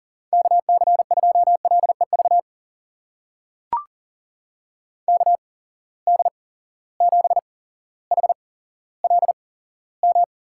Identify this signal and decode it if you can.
Morse code, heard as a keyed tone, about 40 wpm, spaced as for 17 wpm, 700 Hz. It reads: KC2LEV KD7HLM
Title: USA Call Signs - 6 Characters - 8x Character Spacing - 40wpm